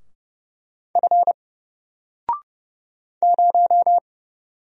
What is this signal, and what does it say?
Morse code, heard as a keyed tone, about 30 wpm, 700 Hz. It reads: F 0
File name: Single-Letter-Number-30wpm